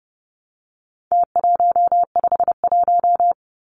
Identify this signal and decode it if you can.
Morse code, heard as a keyed tone, about 30 wpm, 700 Hz. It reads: T151
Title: Numbers - Contest Style - 4-Digit 30wpm